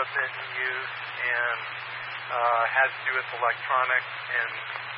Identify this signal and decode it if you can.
Emergency Alert System (SAME) tones, heard as harsh buzzing bursts, with speech. It is heard from 1.1 to 4.7 s.